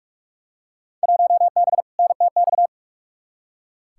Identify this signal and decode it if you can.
Morse code, heard as a keyed tone, about 46 wpm, 700 Hz. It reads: 1B NTX
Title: Field Day Exchanges - 45wpm